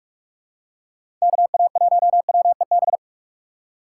Morse code, heard as a keyed tone, about 46 wpm, 700 Hz. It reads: KA1WEB
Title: USA Call Signs - 6 Characters - 8x Character Spacing - 45wpm